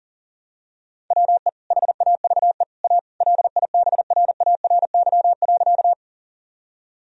Morse code, heard as a keyed tone, about 40 wpm, 700 Hz. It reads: WE HAVE A LIBRARY.
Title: Sentences from Top 1000 Words - Review Part 4 - 40wpm